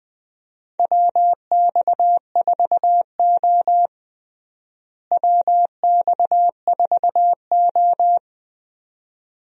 Morse code, heard as a keyed tone, about 20 wpm, 700 Hz. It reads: WX4O WX4O
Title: USA Call Signs - 4 Characters - Repeated 2x - 20wpm